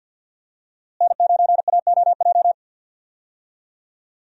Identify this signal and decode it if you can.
Morse code, heard as a keyed tone, about 47 wpm, 700 Hz. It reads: N9AOJ
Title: USA Call Signs - 5 Characters - Repeated 3x - 50wpm